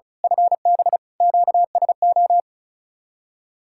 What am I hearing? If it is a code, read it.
Morse code, heard as a keyed tone, about 33 wpm, 700 Hz. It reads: FB QSO
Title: Speed-Racing Common QSO Phrases Encore - 35wpm